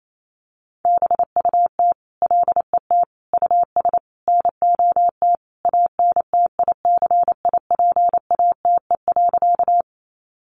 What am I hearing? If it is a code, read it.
Morse code, heard as a keyed tone, about 28 wpm, 700 Hz. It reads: BUT LET US NOT ANTICIPATE.
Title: Sentences from Top 1500 Words 28wpm